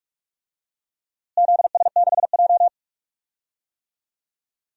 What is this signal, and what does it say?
Morse code, heard as a keyed tone, about 42 wpm, 700 Hz. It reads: ZS6J
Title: DXCC-100 Call Signs - 4 Characters - Repeated 3x - 45wpm